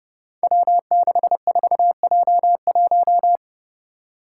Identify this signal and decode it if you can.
Morse code, heard as a keyed tone, about 29 wpm, 700 Hz. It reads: W64J1
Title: Five Challenging Letter-Number Combinations 30wpm